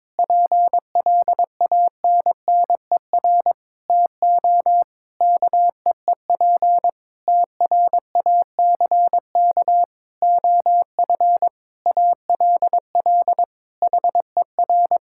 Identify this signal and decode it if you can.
Morse code, heard as a keyed tone, about 22 wpm, 700 Hz. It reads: PLANNER TO KEEP TRACK OF ALL HER
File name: Difficult Sentences with Hints - 22wpm